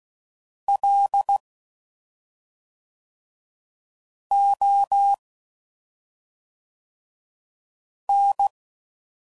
Morse code, heard as a keyed tone, about 16 wpm, 800 Hz. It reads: L O N
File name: Suffixes / Special ING's / SlowToFaster ING's